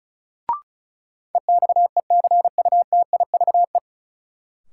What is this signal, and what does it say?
Morse code, heard as a keyed tone, about 36 wpm, 700 Hz. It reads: EXECUTIVE
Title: Sets of 5 Words Encore - New Word Drill - 35wpm